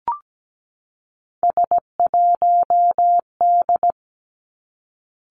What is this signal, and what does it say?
Morse code, heard as a keyed tone, about 17 wpm, 700 Hz. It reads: S1D
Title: Three Challenging Letter-Number Combinations 17wpm